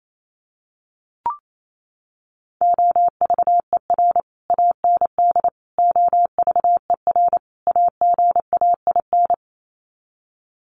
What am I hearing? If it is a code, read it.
Morse code, heard as a keyed tone, about 28 wpm, 700 Hz. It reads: OVER AND OVER AGAIN
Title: Sets of 4 Words 28wpm